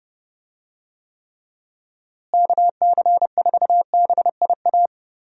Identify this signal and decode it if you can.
Morse code, heard as a keyed tone, about 30 wpm, 700 Hz. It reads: KC4BIA
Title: USA Call Signs - 6 Characters - Repeated 2x - 30wpm